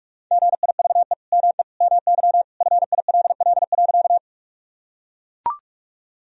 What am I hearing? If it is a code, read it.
Morse code, heard as a keyed tone, about 42 wpm, 700 Hz. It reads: GIVE ME MY FILL.
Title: Sentences from Top 1000 Words - Review Part 3 - 45wpm